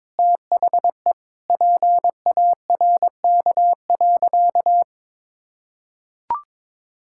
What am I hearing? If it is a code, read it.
Morse code, heard as a keyed tone, about 22 wpm, 700 Hz. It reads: THE PARK.
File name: practice / 22wpm / Sentences with 4-Word Phrases - 22wpm